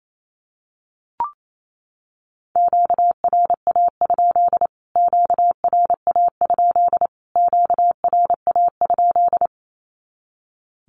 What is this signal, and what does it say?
Morse code, heard as a keyed tone, about 28 wpm, 700 Hz. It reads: QRA? QRA? QRA?
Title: Q-Codes Repeated 3x 28wpm